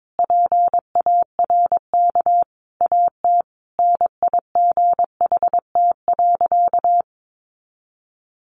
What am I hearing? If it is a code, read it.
Morse code, heard as a keyed tone, about 22 wpm, 700 Hz. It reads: PARK AT NIGHT.
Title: Sentences from Top 1300 Words 22wpm